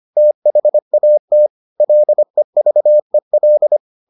Morse code, heard as a keyed tone, about 25 wpm, 600 Hz. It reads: THAT LEVEL